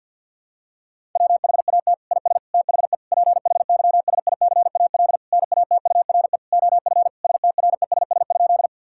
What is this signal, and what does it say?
Morse code, heard as a keyed tone, about 50 wpm, 700 Hz. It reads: WHAT IS THE PHYSICAL NATURE OF STRESS?
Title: Sentences from Top 1000 Words - Review Part 6 - 50wpm